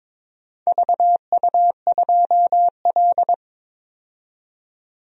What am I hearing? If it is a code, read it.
Morse code, heard as a keyed tone, about 22 wpm, 700 Hz. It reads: VU2L
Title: DXCC-100 Call Signs - 4 Characters - 22wpm